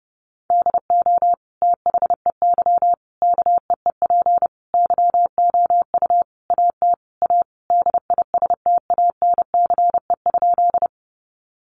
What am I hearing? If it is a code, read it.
Morse code, heard as a keyed tone, about 30 wpm, 700 Hz. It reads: DO THEY KEEP YOU AT A DISTANCE?
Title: Sentences from Top 1200 Words 30wpm